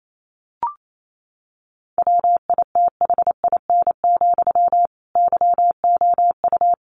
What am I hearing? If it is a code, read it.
Morse code, heard as a keyed tone, about 28 wpm, 700 Hz. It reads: WITHIN, YOU
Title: A Sound Of Thunder 28wpm